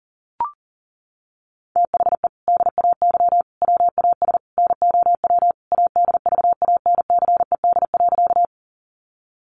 Morse code, heard as a keyed tone, about 37 wpm, 700 Hz. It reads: THE DAY WAS NOW ADVANCED.
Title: Sentences from Top 1200 Words 40wpm